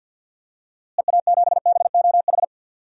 Morse code, heard as a keyed tone, about 52 wpm, 700 Hz. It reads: EA7BOH
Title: DXCC-100 Call Signs - 6 Characters - Repeated 3x - 50wpm